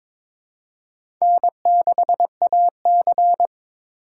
Morse code, heard as a keyed tone, about 22 wpm, 700 Hz. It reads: N6AC
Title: USA Call Signs - 4 Characters - 3x Character Spacing - 22wpm